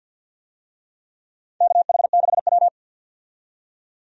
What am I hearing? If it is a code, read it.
Morse code, heard as a keyed tone, about 48 wpm, 700 Hz. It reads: KH6W
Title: DXCC-100 Call Signs - 4 Characters - 50wpm